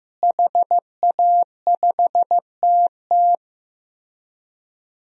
Morse code, heard as a keyed tone, about 15 wpm, 700 Hz. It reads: HA5TT
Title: DXCC-100 Call Signs - 5 Characters - Repeated 3x - 15wpm